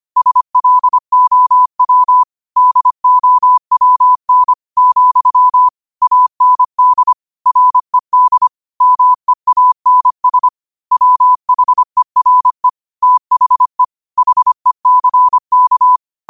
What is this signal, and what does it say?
Morse code, heard as a keyed tone, about 25 wpm, 1000 Hz. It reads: SLOW DOWN , AND RED MEANS WHERE THE HECK